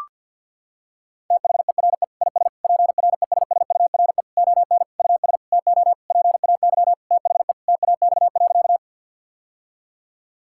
Morse code, heard as a keyed tone, about 50 wpm, 700 Hz. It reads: THERE IS PRESSURE ON US TO PAY THE TAX.
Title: Sentences from Top 1000 Words - Review Part 6 - 50wpm